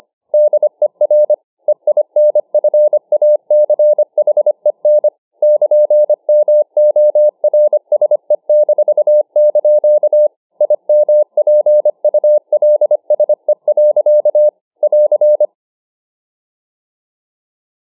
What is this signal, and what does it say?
Morse code, heard as a keyed tone, about 25 wpm, 600 Hz. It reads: DER EINFACHEN <KN>MORSE-) IMPULSE. <AR>